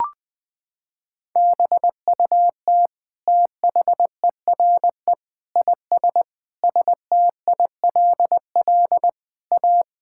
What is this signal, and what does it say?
Morse code, heard as a keyed tone, about 20 wpm, 700 Hz. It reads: BUT THERE IS STILL A
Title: Sets of 9 Words - 20wpm